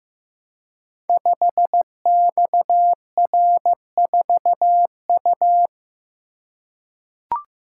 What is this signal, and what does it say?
Morse code, heard as a keyed tone, about 15 wpm, 700 Hz. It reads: 5XR4U
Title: Code Groups - 5 Letter:Numbers - 15wpm